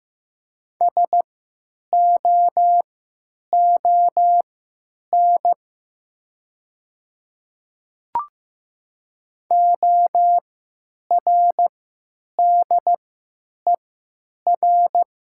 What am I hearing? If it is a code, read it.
Morse code, heard as a keyed tone, about 15 wpm, spaced as for 9 wpm, 700 Hz. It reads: SOON ORDER
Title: 3x Character Spacing - Top 500 Words - 15wpm